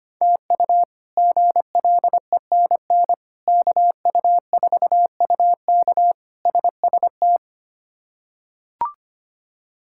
Morse code, heard as a keyed tone, about 25 wpm, 700 Hz. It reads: TU GLENN KU4UK SST
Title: K1USN SST QSO Practice - 25wpm